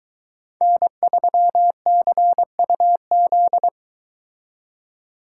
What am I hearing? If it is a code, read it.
Morse code, heard as a keyed tone, about 23 wpm, 700 Hz. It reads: N3CUZ